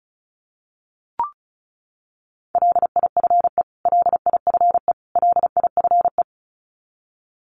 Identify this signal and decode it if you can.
Morse code, heard as a keyed tone, about 34 wpm, 700 Hz. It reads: LIFE LIFE LIFE
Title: Top 500 Words Repeated 35wpm